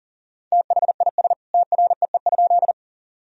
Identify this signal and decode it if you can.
Morse code, heard as a keyed tone, about 40 wpm, 700 Hz. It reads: THIS TREE?
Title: Sentences from Top 1000 Words - Review Part 1 - 40wpm